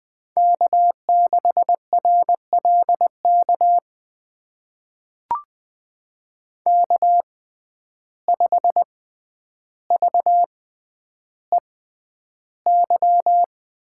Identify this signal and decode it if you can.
Morse code, heard as a keyed tone, about 21 wpm, spaced as for 10 wpm, 700 Hz. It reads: K6RLK K5VEY
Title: USA Call Signs - 5 Characters - 6x Character Spacing - 20wpm